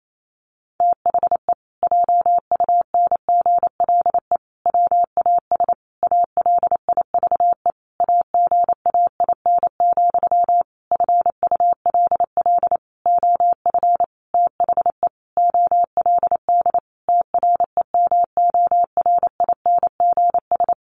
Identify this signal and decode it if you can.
Morse code, heard as a keyed tone, about 28 wpm, 700 Hz. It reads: THE JUNGLE WAS ALIVE AGAIN, FULL OF THE OLD TREMORINGS
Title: A Sound Of Thunder 28wpm